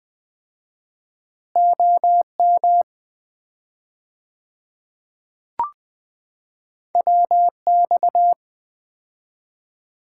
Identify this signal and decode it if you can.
Morse code, heard as a keyed tone, about 20 wpm, 700 Hz. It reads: OM WX
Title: Common QSO Phrases - 4x Word-Spacing - 20wpm